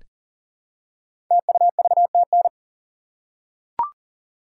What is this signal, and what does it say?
Morse code, heard as a keyed tone, about 40 wpm, 700 Hz. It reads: TUVTN
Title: Code Groups - 5 Letters - 40wpm